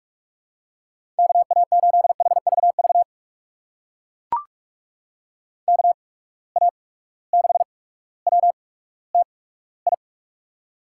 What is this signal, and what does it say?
Morse code, heard as a keyed tone, about 44 wpm, spaced as for 19 wpm, 700 Hz. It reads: KA8HVV KA6WTI
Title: USA Call Signs - 6 Characters - 8x Character Spacing - 45wpm